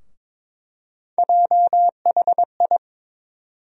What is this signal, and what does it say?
Morse code, heard as a keyed tone, about 21 wpm, 700 Hz. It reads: JHI
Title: USA Call Sign Postfixes - 3 Letters 22wpm